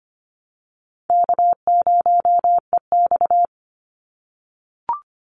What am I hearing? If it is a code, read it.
Morse code, heard as a keyed tone, about 25 wpm, 700 Hz. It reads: K0EX
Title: Call Signs - 25wpm